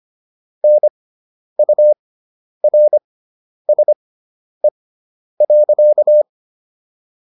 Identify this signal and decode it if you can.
Morse code, heard as a keyed tone, about 25 wpm, spaced as for 10 wpm, 600 Hz. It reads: NURSE.